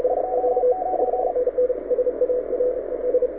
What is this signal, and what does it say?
Morse code, heard as a keyed tone, about 35 wpm, 740 Hz, with radio noise. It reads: 29
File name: audio zawody / IARU 2015 CW SN7Q